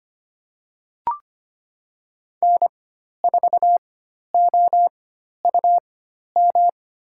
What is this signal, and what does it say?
Morse code, heard as a keyed tone, about 25 wpm, spaced as for 12 wpm, 700 Hz. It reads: N4OUM
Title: USA Call Signs - 4x Character Spacing - 25wpm